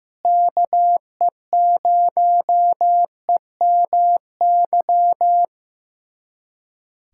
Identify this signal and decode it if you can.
Morse code, heard as a keyed tone, about 15 wpm, 700 Hz. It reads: KE0EMY
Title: USA Callsign DE Callsign 15wpm